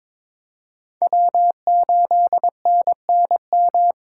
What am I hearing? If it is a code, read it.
Morse code, heard as a keyed tone, about 23 wpm, 700 Hz. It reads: W8NNM
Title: USA Call Signs - 5 Characters - 3x Character Spacing - 22wpm